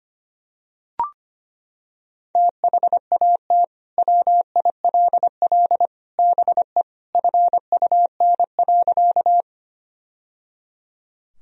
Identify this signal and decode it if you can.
Morse code, heard as a keyed tone, about 25 wpm, 700 Hz. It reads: THAT WILL BE FUN.
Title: Sentences from Top 1000 Words - Review Part 4 - 25wpm